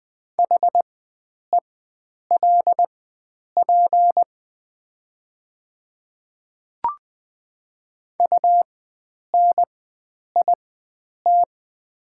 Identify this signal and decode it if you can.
Morse code, heard as a keyed tone, about 20 wpm, spaced as for 9 wpm, 690 Hz. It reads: HELP UNIT
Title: 4x Character Spacing - Top 500 Words - 20wpm